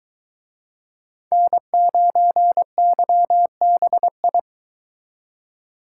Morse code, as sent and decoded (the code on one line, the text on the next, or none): -. ----. -.-- -... ..
N9YBI